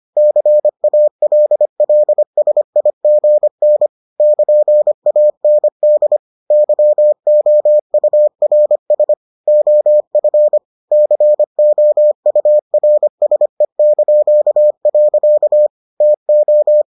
Morse code, heard as a keyed tone, about 25 wpm, 600 Hz. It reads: CALLSIGN <KN>AND YOURS OF COURSE). TO